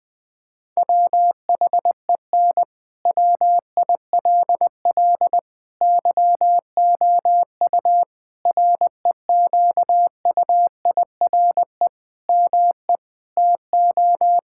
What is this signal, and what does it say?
Morse code, heard as a keyed tone, about 20 wpm, 700 Hz. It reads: WHEN WILL YOU REQUIRE ME TO
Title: Sentences from Top 1000 Words - Review Part 6 - 20wpm